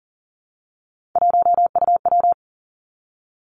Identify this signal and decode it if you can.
Morse code, heard as a keyed tone, about 38 wpm, 700 Hz. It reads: 1UW
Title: Three Challenging Letter-Number Combinations 40wpm